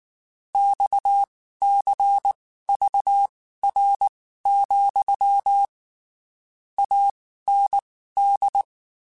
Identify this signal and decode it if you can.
Morse code, heard as a keyed tone, about 19 wpm, spaced as for 14 wpm, 800 Hz. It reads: XCVR, AND
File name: QSOs Spaced